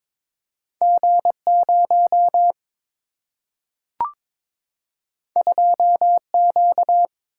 Morse code, heard as a keyed tone, about 22 wpm, 700 Hz. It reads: G0 2Q
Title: Single-Letter Single-Number 22wpm